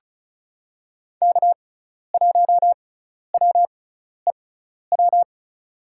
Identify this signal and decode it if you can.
Morse code, heard as a keyed tone, about 34 wpm, spaced as for 12 wpm, 700 Hz. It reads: K1WEW
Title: USA Call Signs - 5 Characters - 6x Character Spacing - 35wpm